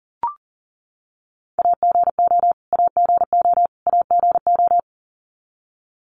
Morse code, heard as a keyed tone, about 39 wpm, 700 Hz. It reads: AGO AGO AGO
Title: Top 500 Words Repeated 40wpm